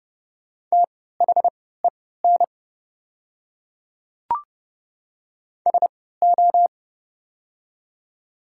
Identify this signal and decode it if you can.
Morse code, heard as a keyed tone, about 30 wpm, spaced as for 17 wpm, 700 Hz. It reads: THEN SO
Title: 3x Character Spacing - Top 100 Words - 30wpm